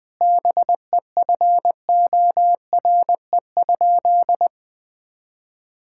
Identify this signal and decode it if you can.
Morse code, heard as a keyed tone, about 20 wpm, 700 Hz. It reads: BEFORE?
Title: Sentences from Top 1000 Words - Review Part 6 - 20wpm